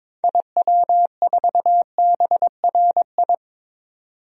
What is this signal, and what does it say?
Morse code, heard as a keyed tone, about 22 wpm, 700 Hz. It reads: IW4BRI
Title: DXCC-100 Call Signs - 6 Characters - 22wpm